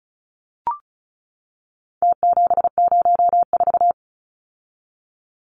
Morse code, heard as a keyed tone, about 34 wpm, 700 Hz. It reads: T704
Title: Numbers - Contest Style - 4-Digit 35wpm